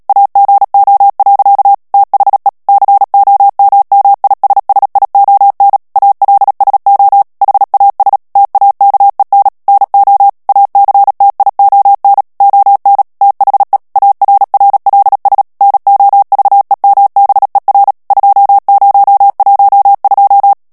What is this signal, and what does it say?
Morse code, heard as a keyed tone, about 37 wpm, 800 Hz. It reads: AGO. THE COMMISSION ALSO HAS TAKEN NO ACTION ON THE ARRLS NOVEMBER 2012
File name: morse mp3 / ANGLAIS texte clair 3 vitesse 35 mots minute